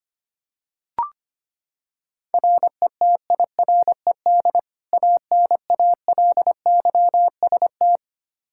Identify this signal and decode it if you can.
Morse code, heard as a keyed tone, about 25 wpm, 700 Hz. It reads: RETIRED ANALYST
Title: K1USN SST QSO Practice 25wpm